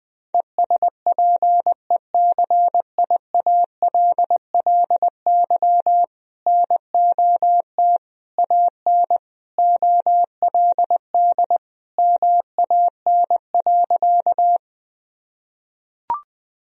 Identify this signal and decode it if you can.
Morse code, heard as a keyed tone, about 21 wpm, 700 Hz. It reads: ESPECIALLY NOT AN OLD MAN.
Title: Sentences from Top 1000 Words - Review Part 4 - 20wpm